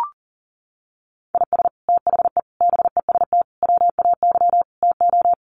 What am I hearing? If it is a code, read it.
Morse code, heard as a keyed tone, about 40 wpm, 700 Hz. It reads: IS THE BEST WAY TO
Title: Sets of 5 Words Encore 40wpm